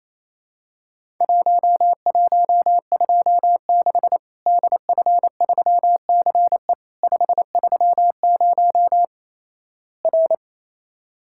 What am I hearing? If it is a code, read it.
Morse code, heard as a keyed tone, about 28 wpm, 700 Hz. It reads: 1126 DF3CE 530 R
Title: Worked All Europe DX Contest - QTC Exchange - 28wpm